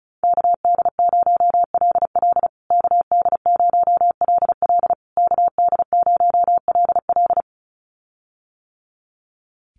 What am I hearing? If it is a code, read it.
Morse code, heard as a keyed tone, about 34 wpm, 700 Hz. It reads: KD0LL KD0LL KD0LL
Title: USA Call Signs Repeated 3x 35wpm